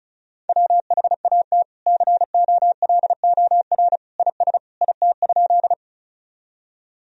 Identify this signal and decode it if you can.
Morse code, heard as a keyed tone, about 34 wpm, 700 Hz. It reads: WHAT COLOR IS IT?
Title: Sentences from Top 1000 Words - Review Part 2 - 35wpm